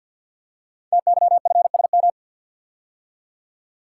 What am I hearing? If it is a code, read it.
Morse code, heard as a keyed tone, about 50 wpm, 690 Hz. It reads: TYFSM
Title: Code Groups - 5 Letters - 50wpm